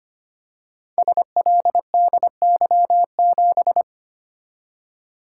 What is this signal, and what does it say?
Morse code, heard as a keyed tone, about 25 wpm, 700 Hz. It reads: SLDY7
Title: Code Groups - 5 Letter:Numbers - 25wpm